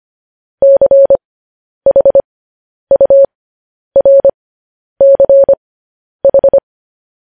Morse code, heard as a keyed tone, about 25 wpm, spaced as for 10 wpm, 550 Hz. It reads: CHURCH
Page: news headlines in morse code at 10 wpm